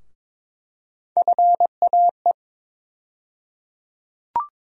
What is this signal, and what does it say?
Morse code, heard as a keyed tone, about 22 wpm, 700 Hz. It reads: FAE
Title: USA Call Sign Postfixes - 3 Letters 22wpm